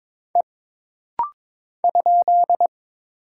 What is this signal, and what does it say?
Morse code, heard as a keyed tone, about 22 wpm, 700 Hz. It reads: E ?
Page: This rapid-fire timing is geared to push your ICR (Instant Character Recognition) skills.